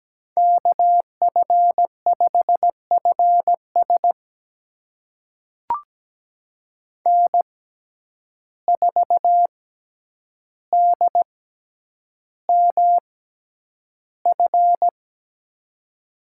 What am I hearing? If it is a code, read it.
Morse code, heard as a keyed tone, about 17 wpm, spaced as for 9 wpm, 700 Hz. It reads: KF5FS N4DMF
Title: USA Call Signs - 5 Characters - 6x Character Spacing - 17wpm